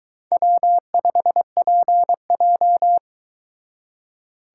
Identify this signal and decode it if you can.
Morse code, heard as a keyed tone, about 22 wpm, 700 Hz. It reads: W5PJ